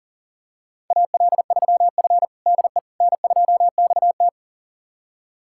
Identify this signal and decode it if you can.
Morse code, heard as a keyed tone, about 40 wpm, 700 Hz. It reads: AL3F DE N2XT
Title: USA Callsign DE Callsign - 4 Characters - 40wpm